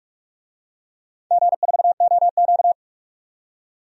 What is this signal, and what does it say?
Morse code, heard as a keyed tone, about 44 wpm, 700 Hz. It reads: G4OQ
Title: DXCC-100 Call Signs - 4 Characters - 45wpm